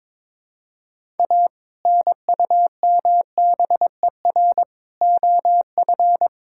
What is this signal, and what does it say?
Morse code, heard as a keyed tone, about 21 wpm, 700 Hz. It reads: A NUMBER OF
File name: practice / 22wpm / Sentences with 2-Word Phrases - 22wpm